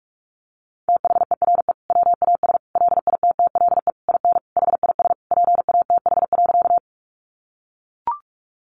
Morse code, heard as a keyed tone, about 44 wpm, 700 Hz. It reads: THERE WAS LITTLE IN HIS PATH.
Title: Sentences from Top 1400 Words 45wpm